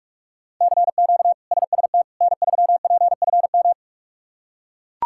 Morse code, heard as a keyed tone, about 41 wpm, 700 Hz. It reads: CQ SST N3PFM
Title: K1USN SST QSO Practice - 45wpm